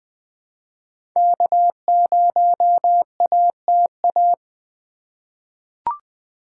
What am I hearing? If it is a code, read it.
Morse code, heard as a keyed tone, about 20 wpm, 700 Hz. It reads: K0ATA
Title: USA Call Signs - 5 Characters - 3x Character Spacing - 20wpm